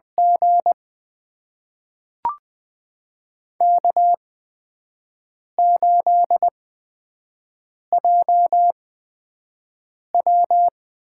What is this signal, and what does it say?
Morse code, heard as a keyed tone, about 20 wpm, spaced as for 5 wpm, 700 Hz. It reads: G K8JW
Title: USA Call Signs - 4 Characters - 8x Character Spacing - 20wpm